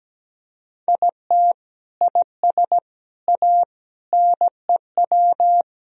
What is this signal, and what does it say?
Morse code, heard as a keyed tone, about 17 wpm, 700 Hz. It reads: IT IS A NEW
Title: Sentences from Top 1000 Words - Review Part 3 - 17wpm